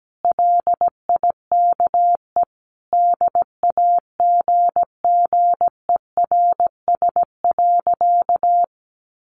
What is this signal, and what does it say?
Morse code, heard as a keyed tone, about 17 wpm, 700 Hz. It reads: LIKE DAGGERS.
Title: A Sound Of Thunder 17wpm